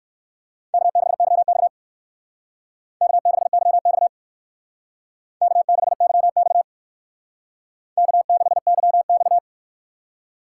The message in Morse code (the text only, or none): K6YX K6YX K6YX K6YX